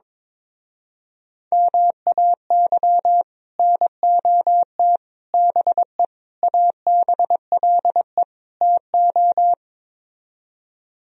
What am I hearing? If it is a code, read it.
Morse code, heard as a keyed tone, about 23 wpm, 700 Hz. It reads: MAY NOT BE ABLE TO
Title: Speed-Racing - Sets of 5 Words 22wpm